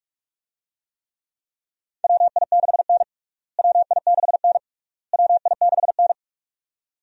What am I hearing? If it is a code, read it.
Morse code, heard as a keyed tone, about 45 wpm, 700 Hz. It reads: WI6N WI6N WI6N
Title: USA Call Signs - 4 Characters - Repeated 3x - 45wpm